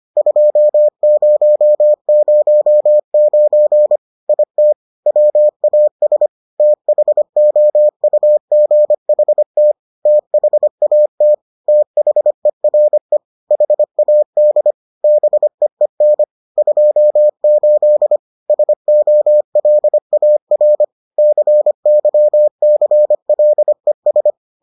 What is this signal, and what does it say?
Morse code, heard as a keyed tone, about 25 wpm, 600 Hz. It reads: 2009 IT WAS THOUGHT THAT THERE HAD BEEN 28 SOLAR CYCLES